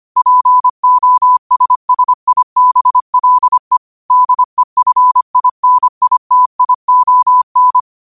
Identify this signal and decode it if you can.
Morse code, heard as a keyed tone, about 25 wpm, 1000 Hz. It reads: POSSIBLE DEFINITION
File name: qotdfiles / 25wpm